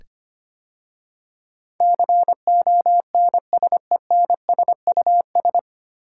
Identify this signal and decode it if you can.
Morse code, heard as a keyed tone, about 25 wpm, 700 Hz. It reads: CONSENSUS
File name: practice / 25wpm / 9 Letter Words - First 3 Letters Given - 25wpm